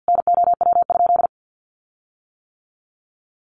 Morse code, heard as a keyed tone, about 50 wpm, 700 Hz. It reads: NOW?
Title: Sentences from Top 1500 Words 50wpm